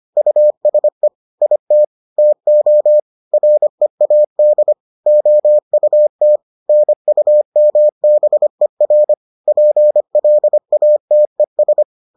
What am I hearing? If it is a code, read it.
Morse code, heard as a keyed tone, about 25 wpm, 600 Hz. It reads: USE IT TO READ OUT NUMBER PLATES